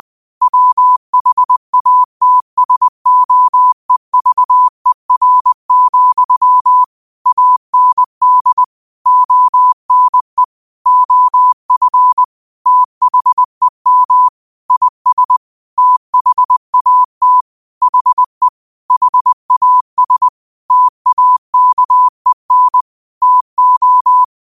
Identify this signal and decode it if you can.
Morse code, heard as a keyed tone, about 20 wpm, 1000 Hz. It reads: WHATSOEVER, AND ONE OF THEM IS THAT HE HAS TAKEN TO
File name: qotdfiles / 20wpm